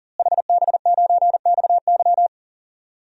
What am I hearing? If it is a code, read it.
Morse code, heard as a keyed tone, about 37 wpm, 700 Hz. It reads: HB9XY
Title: DXCC-100 Call Signs - 5 Characters - Repeated 3x - 40wpm